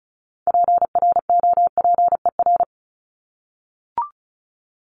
Morse code, heard as a keyed tone, about 34 wpm, 700 Hz. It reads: PROPER
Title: Top 751-1000 Words 35wpm